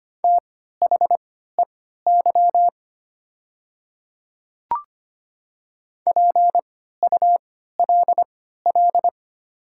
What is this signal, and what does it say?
Morse code, heard as a keyed tone, about 25 wpm, spaced as for 14 wpm, 700 Hz. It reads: THEY PULL
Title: 3x Character Spacing - Top 500 Words - 25wpm